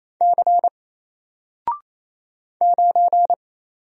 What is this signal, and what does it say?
Morse code, heard as a keyed tone, about 27 wpm, 700 Hz. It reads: C 9
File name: Single-Letter-Number-28wpm